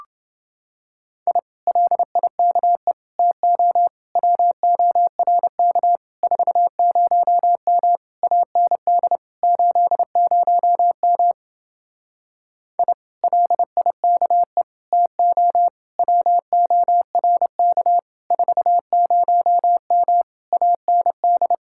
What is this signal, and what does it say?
Morse code, heard as a keyed tone, about 28 wpm, 700 Hz. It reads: I LIKE TO WORK 40M AND 80M I LIKE TO WORK 40M AND